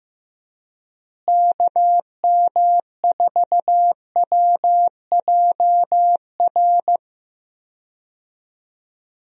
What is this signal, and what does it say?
Morse code, heard as a keyed tone, about 15 wpm, 700 Hz. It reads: KM4WJR
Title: USA Call Signs - 6 Characters - Repeated 2x - 15wpm